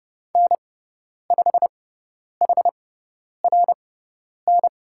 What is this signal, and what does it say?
Morse code, heard as a keyed tone, about 30 wpm, spaced as for 10 wpm, 700 Hz. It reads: N5HRN
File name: Lesson 16g - Letter c - Call Signs 30wpm